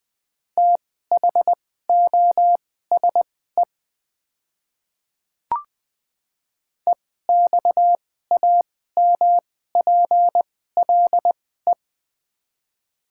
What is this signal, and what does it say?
Morse code, heard as a keyed tone, about 20 wpm, spaced as for 14 wpm, 700 Hz. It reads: THOSE EXAMPLE
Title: 2x Character Spacing - Top 300 Words - 20wpm